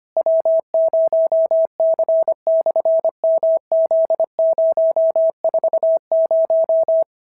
Morse code, heard as a keyed tone, about 23 wpm, 650 Hz. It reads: W0C/MZ040
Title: SOTA QSO Practice 25wpm